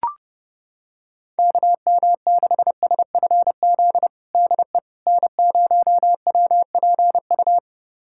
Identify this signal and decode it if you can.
Morse code, heard as a keyed tone, about 30 wpm, 700 Hz. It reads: KM6SFZ DE N0WPU
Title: USA Callsign DE Callsign 30wpm